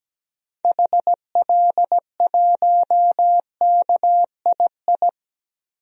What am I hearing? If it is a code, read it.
Morse code, heard as a keyed tone, about 17 wpm, 700 Hz. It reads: HL1KII
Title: DXCC-100 Call Signs - 6 Characters - Repeated 2x - 17wpm